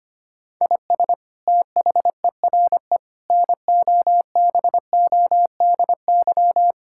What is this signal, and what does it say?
Morse code, heard as a keyed tone, about 25 wpm, 700 Hz. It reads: IS THERE NOBODY
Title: Sentences from Top 1600 Words 25wpm